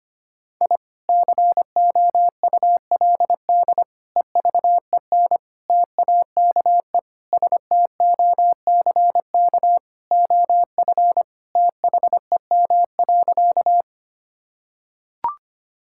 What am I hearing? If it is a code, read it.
Morse code, heard as a keyed tone, about 25 wpm, 700 Hz. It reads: I COULD EVEN TAKE STOCK OF THEM.
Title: Sentences from Top 1000 Words - Review Part 5 - 25wpm